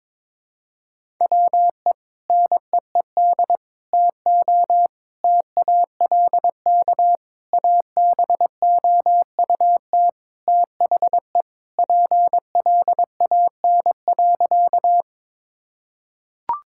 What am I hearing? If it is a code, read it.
Morse code, heard as a keyed tone, about 22 wpm, 700 Hz. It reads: WE NEED TO TALK ABOUT THE PLAN.
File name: practice / 22wpm / Sentences with 4-Word Phrases - 22wpm